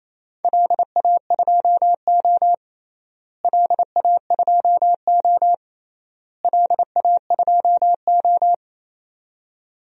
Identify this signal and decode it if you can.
Morse code, heard as a keyed tone, about 28 wpm, 700 Hz. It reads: LA2O LA2O LA2O
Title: DXCC-100 Call Signs - 4 Characters - Repeated 3x - 28wpm